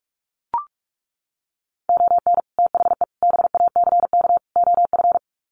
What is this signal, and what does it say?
Morse code, heard as a keyed tone, about 44 wpm, 700 Hz. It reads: ON THE BACK OF
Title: Sets of 4 Words 45wpm